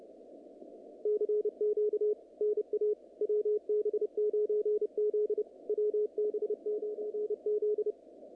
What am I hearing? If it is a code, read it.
Morse code, heard as a keyed tone, about 30 wpm, 440 Hz, with radio noise. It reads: CQ NA WB9Z WB9Z
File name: contests / naqp-cw-k5tr-op-k5pi- -2200